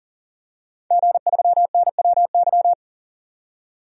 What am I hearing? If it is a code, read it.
Morse code, heard as a keyed tone, about 40 wpm, 700 Hz. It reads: G3NWY
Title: DXCC-100 Call Signs - 5 Characters - 40wpm